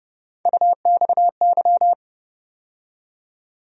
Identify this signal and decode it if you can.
Morse code, heard as a keyed tone, about 31 wpm, 700 Hz. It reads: UXY
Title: USA Call Sign Postfixes - 3 Letters 30wpm